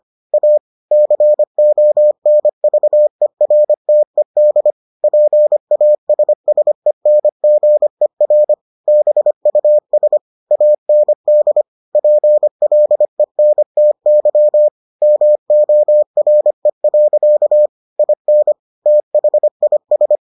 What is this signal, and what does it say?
Morse code, heard as a keyed tone, about 25 wpm, 600 Hz. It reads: A CONVERTED PASSENGER BUS AND PLENTY MORE. IN THIS